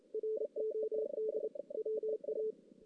Morse code, heard as a keyed tone, about 24 wpm, 410 Hz. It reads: ROD WA
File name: 2016 JAN CW NAQP / 2016 Jan naqp k5ot - 0-5